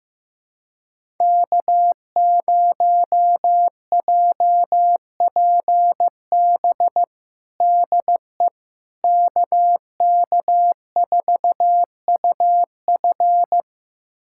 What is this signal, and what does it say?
Morse code, heard as a keyed tone, about 15 wpm, 700 Hz. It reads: K0JPB DE KK4UF
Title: USA Callsign DE Callsign - 5 Characters - 15wpm